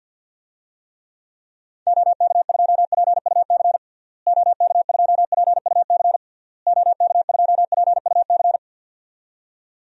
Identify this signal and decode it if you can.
Morse code, heard as a keyed tone, about 51 wpm, 700 Hz. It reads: OK2PUC OK2PUC OK2PUC
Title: DXCC-100 Call Signs - 6 Characters - Repeated 3x - 50wpm